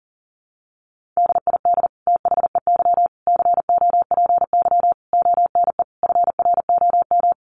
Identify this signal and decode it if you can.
Morse code, heard as a keyed tone, about 40 wpm, 700 Hz. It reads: DID THEY COPY ONE FROM
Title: Sentences from Top 1400 Words 40wpm